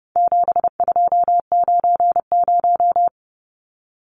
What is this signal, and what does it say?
Morse code, heard as a keyed tone, about 30 wpm, 700 Hz. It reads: 7290
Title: Numbers - 4-Digit 30wpm